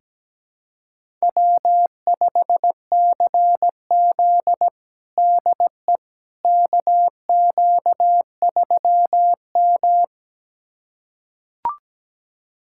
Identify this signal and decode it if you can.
Morse code, heard as a keyed tone, about 17 wpm, 700 Hz. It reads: W5CZ DE KQ3M
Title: USA Callsign DE Callsign - 4 Characters - 17wpm